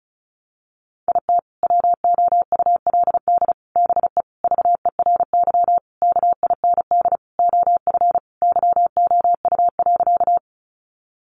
Morse code, heard as a keyed tone, about 34 wpm, 700 Hz. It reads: IT WOULD BE VERY KIND OF YOU.
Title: Sentences from Top 200 Words 35wpm